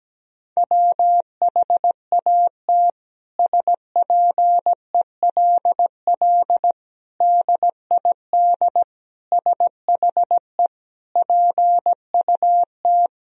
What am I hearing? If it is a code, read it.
Morse code, heard as a keyed tone, about 17 wpm, 700 Hz. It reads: WHAT SPELL DID SHE PUT
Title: Sentences from Top 1000 Words - Review Part 1 - 17wpm